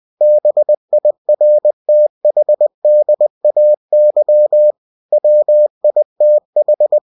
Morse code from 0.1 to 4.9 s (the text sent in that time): BIRTHDAY